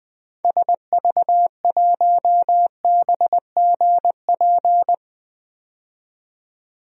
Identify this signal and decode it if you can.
Morse code, heard as a keyed tone, about 20 wpm, 700 Hz. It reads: SV1BGP
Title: DXCC-100 Call Signs - 6 Characters - 20wpm